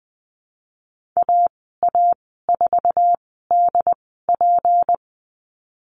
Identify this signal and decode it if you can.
Morse code, heard as a keyed tone, about 20 wpm, spaced as for 15 wpm, 700 Hz. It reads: AA4DP
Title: 193 Lesson 31g - Letter v - Call Signs 20wpm